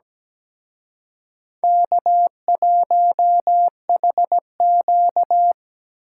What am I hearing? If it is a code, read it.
Morse code, heard as a keyed tone, about 20 wpm, 700 Hz. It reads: K1HQ
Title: Speed Racing - USA 1x2 Call Signs 17wpm